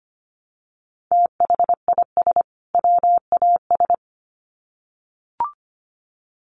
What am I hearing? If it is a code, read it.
Morse code, heard as a keyed tone, about 25 wpm, 700 Hz. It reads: THIS WAS
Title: Sets of 2 Words 25wpm